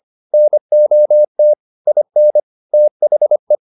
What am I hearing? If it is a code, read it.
Morse code, heard as a keyed tone, about 25 wpm, 600 Hz. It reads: NOT IN THE
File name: the old man and the sea part 1 25wpm